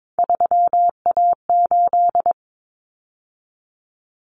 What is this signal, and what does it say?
Morse code, heard as a keyed tone, about 22 wpm, 700 Hz. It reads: 3A8
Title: Common Cut Numbers - 3-Digit 22wpm